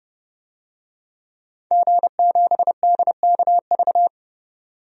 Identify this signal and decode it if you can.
Morse code, heard as a keyed tone, about 32 wpm, 700 Hz. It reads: G7DKV
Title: DXCC-100 Call Signs - 5 Characters - 30wpm